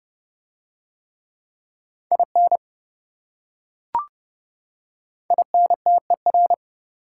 Morse code, heard as a keyed tone, about 28 wpm, 700 Hz. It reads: IN INTER